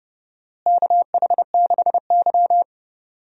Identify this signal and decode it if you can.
Morse code, heard as a keyed tone, about 30 wpm, 700 Hz. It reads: KH6Y
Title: DXCC-100 Call Signs - 4 Characters - Repeated 2x - 30wpm